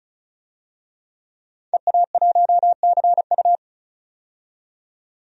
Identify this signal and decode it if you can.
Morse code, heard as a keyed tone, about 36 wpm, 700 Hz. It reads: EA1CU
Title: DXCC-100 Call Signs - 5 Characters - 35wpm